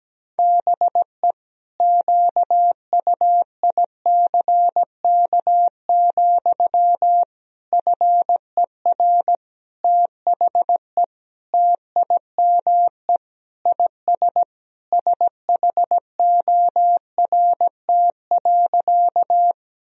Morse code, heard as a keyed tone, about 17 wpm, 700 Hz. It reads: BE QUICK, FER THE TIME IS SHORT.
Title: Sentences from Top 500 Words and Abbreviations 17wpm